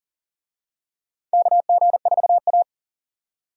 Morse code, heard as a keyed tone, about 38 wpm, 700 Hz. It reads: KG4A
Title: USA Call Signs - 4 Characters - 40wpm